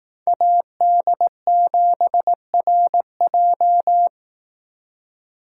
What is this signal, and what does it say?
Morse code, heard as a keyed tone, about 18 wpm, 700 Hz. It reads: AD7RJ
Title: Speed Racing - USA 2x2 Call Signs 15wpm